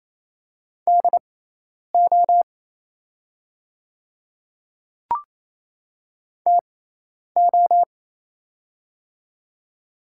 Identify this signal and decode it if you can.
Morse code, heard as a keyed tone, about 28 wpm, spaced as for 10 wpm, 700 Hz. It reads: DO TO
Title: 6x Character Spacing - Top 100 Words - 28wpm